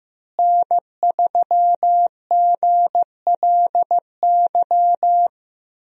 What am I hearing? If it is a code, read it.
Morse code, heard as a keyed tone, about 15 wpm, 700 Hz. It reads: N3GLY
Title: USA Call Signs - 5 Characters - Repeated 3x - 15wpm